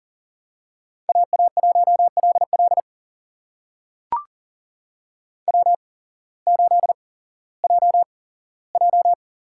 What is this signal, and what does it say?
Morse code, heard as a keyed tone, about 39 wpm, spaced as for 19 wpm, 700 Hz. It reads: AA1LL W8JJ
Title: USA Call Signs - 5 Characters - 8x Character Spacing - 40wpm